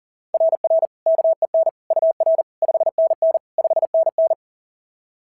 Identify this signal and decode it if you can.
Morse code, heard as a keyed tone, about 40 wpm, 690 Hz. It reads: RR KEN UR 5NN 5NN
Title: SOTA QSO Practice 40wpm